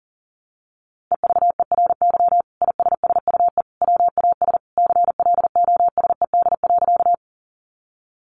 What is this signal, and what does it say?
Morse code, heard as a keyed tone, about 40 wpm, 700 Hz. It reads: EVERY ISSUE WAS CLOSED.
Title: Sentences from Top 1400 Words 40wpm